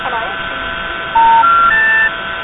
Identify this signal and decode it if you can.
Special information tone, with speech.
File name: Verizon-centralized-intercept-bad-trunk-with-MF-and-crosstalk